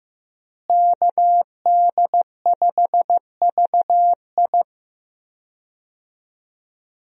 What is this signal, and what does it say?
Morse code, heard as a keyed tone, about 15 wpm, 700 Hz. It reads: KD5VI
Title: USA Call Signs - 5 Characters - Repeated 3x - 15wpm